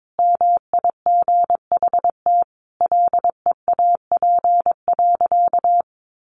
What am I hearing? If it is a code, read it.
Morse code, heard as a keyed tone, about 22 wpm, 700 Hz. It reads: MIGHT LEAP.
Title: A Sound Of Thunder 22wpm